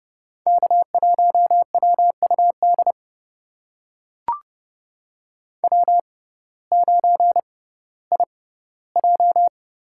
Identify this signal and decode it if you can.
Morse code, heard as a keyed tone, about 30 wpm, spaced as for 17 wpm, 700 Hz. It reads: K1WUD W9IJ
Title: USA Call Signs - 6x Character Spacing - 30wpm